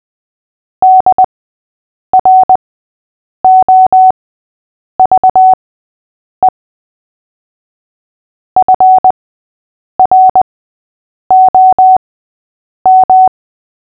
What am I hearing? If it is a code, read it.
Morse code, heard as a keyed tone, about 20 wpm, spaced as for 8 wpm, 750 Hz. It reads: DROVE FROM
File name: cw / fists / Short-Story-5-8wpm